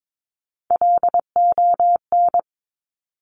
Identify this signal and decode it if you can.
Morse code, heard as a keyed tone, about 22 wpm, 700 Hz. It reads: LON
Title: Top Three Letter Combinations in Words 22wpm